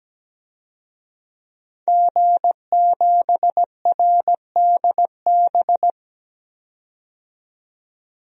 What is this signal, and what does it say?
Morse code, heard as a keyed tone, about 17 wpm, 700 Hz. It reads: G7RDB
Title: DXCC-100 Call Signs - 5 Characters - 17wpm